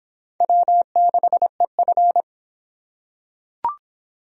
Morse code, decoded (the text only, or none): W6EF